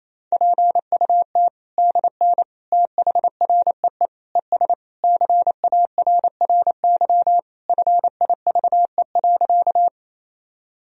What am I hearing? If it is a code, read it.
Morse code, heard as a keyed tone, about 27 wpm, 700 Hz. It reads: PUT DN THREE ES CARRY FIVE.
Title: Sentences from Top 400 Words and Abbreviations 28wpm